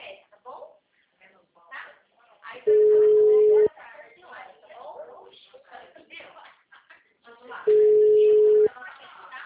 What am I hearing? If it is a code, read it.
Ringback tone (Europe, 425 Hz), with speech.